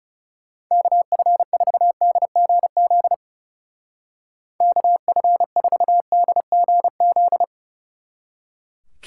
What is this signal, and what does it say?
Morse code, heard as a keyed tone, about 36 wpm, 700 Hz. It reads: KF4DGZ KF4DGZ
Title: Speed Racing - USA 2x3 Call Signs 30wpm